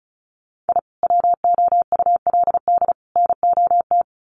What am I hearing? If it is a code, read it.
Morse code, heard as a keyed tone, about 34 wpm, 700 Hz. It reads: I WOULD NOT
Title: Sets of 3 Words 35wpm